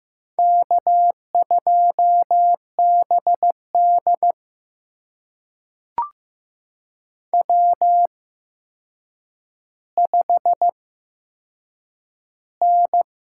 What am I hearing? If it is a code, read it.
Morse code, heard as a keyed tone, about 15 wpm, spaced as for 7 wpm, 700 Hz. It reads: K2BD W5N
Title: USA Call Signs - 4 Characters - 8x Character Spacing - 15wpm